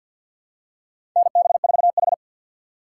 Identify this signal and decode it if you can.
Morse code, heard as a keyed tone, about 51 wpm, 700 Hz. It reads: NB4H
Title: USA Call Signs - 4 Characters - 3x Character Spacing - 50wpm